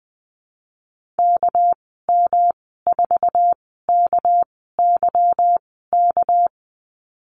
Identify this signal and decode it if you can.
Morse code, heard as a keyed tone, about 20 wpm, spaced as for 15 wpm, 700 Hz. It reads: KM4KYK
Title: 200 Lesson 32g - Letter k - Call Signs 20wpm